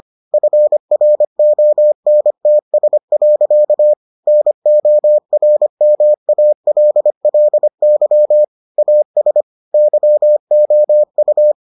Morse code, heard as a keyed tone, about 25 wpm, 600 Hz. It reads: FRONTS. NORMALLY AS YOU